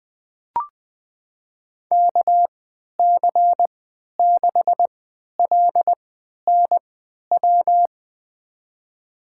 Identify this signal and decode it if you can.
Morse code, heard as a keyed tone, about 20 wpm, spaced as for 11 wpm, 700 Hz. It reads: KC6LNW
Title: USA Call Signs - 6 Characters - 3x Character Spacing - 20wpm